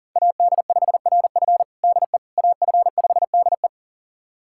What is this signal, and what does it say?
Morse code, heard as a keyed tone, about 40 wpm, 690 Hz. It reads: AD5RF DE AF5DE
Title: USA Callsign DE Callsign - 5 Characters - 40wpm